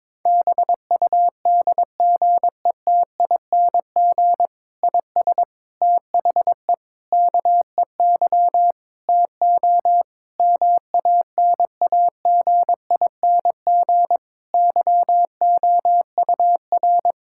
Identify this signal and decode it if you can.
Morse code, heard as a keyed tone, about 21 wpm, 700 Hz. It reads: BUDGETING IS THE KEY TO MANAGING YOUR
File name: Difficult Sentences with Hints - 22wpm